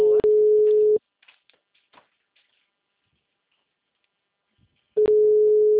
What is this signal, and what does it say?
Ringback tone (Europe, 425 Hz).
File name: out- -2033- - - .9252